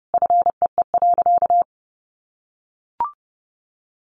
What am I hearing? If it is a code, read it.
Morse code, heard as a keyed tone, about 30 wpm, 700 Hz. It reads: FEE.
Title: Sentences from Top 1500 Words 30wpm